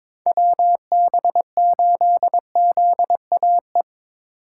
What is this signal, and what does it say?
Morse code, heard as a keyed tone, about 22 wpm, 700 Hz. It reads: WB8ZAE
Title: USA Call Signs - 6 Characters - 2x Character Spacing - 22wpm